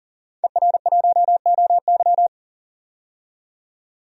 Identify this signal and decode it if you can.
Morse code, heard as a keyed tone, about 38 wpm, 700 Hz. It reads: ER1OY
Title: DXCC-100 Call Signs - 5 Characters - 40wpm